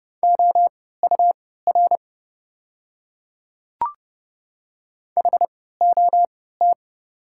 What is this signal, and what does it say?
Morse code, heard as a keyed tone, about 32 wpm, spaced as for 17 wpm, 690 Hz. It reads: OUR HOT
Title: 3x Character Spacing - Top 300 Words - 30wpm